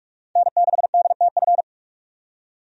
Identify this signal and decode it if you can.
Morse code, heard as a keyed tone, about 44 wpm, 700 Hz. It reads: N6DTF
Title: USA Call Signs - 5 Characters - 45wpm